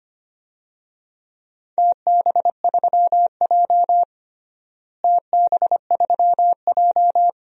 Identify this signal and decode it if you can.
Morse code, heard as a keyed tone, about 25 wpm, 700 Hz. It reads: TB3J TB3J
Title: DXCC-100 Call Signs - 4 Characters - Repeated 2x - 25wpm